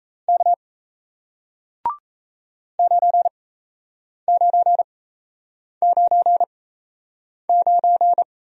Morse code, heard as a keyed tone, about 38 wpm, spaced as for 8 wpm, 700 Hz. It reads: K 9999